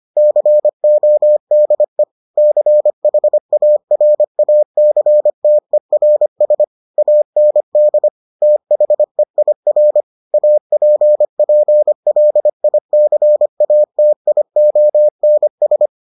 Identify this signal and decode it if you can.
Morse code, heard as a keyed tone, about 25 wpm, 600 Hz. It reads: CODE CHARACTERS AND THEIR APPLICATIONS